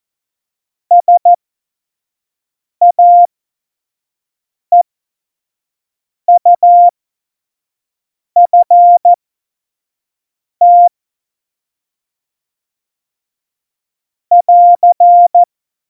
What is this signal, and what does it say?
Morse code, heard as a keyed tone, about 14 wpm, spaced as for 5 wpm, 700 Hz. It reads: SAEUFT <AR>
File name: Morsekurs / Witze / Witz 14 Juli 2020, 25 F70